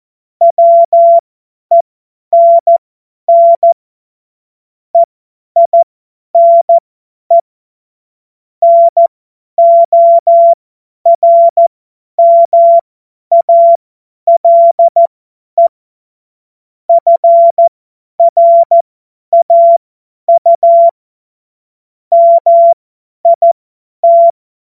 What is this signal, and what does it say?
Morse code, heard as a keyed tone, about 14 wpm, spaced as for 10 wpm, 680 Hz. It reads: WENN EINE NORMALE FRAU MIT
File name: Morsekurs / Witze / Witz 27. April 2021, 50 F70